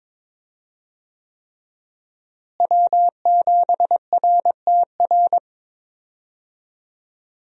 Morse code, heard as a keyed tone, about 22 wpm, 700 Hz. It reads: W7RTR
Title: USA Call Signs - 5 Characters - Repeated 3x - 22wpm